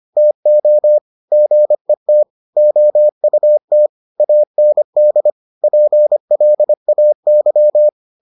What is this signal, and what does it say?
Morse code, heard as a keyed tone, about 25 wpm, 600 Hz. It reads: TO GET OUT AND PLAY